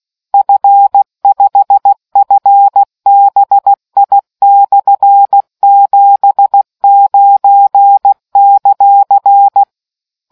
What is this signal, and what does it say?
Morse code, heard as a keyed tone, about 16 wpm, 800 Hz. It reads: F5FBI/79;